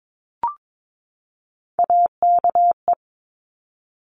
Morse code, heard as a keyed tone, about 22 wpm, 700 Hz. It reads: AKE
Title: Top Three Letter Combinations in Words 22wpm